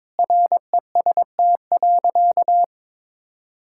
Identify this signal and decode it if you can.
Morse code, heard as a keyed tone, about 22 wpm, 700 Hz. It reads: REST.
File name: practice / 22wpm / Sentences with 8-Word Phrases - 22wpm